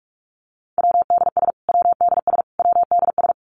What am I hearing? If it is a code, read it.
Morse code, heard as a keyed tone, about 46 wpm, 700 Hz. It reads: WDS WDS WDS
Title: Abbreviations Repeated 3x - 45wpm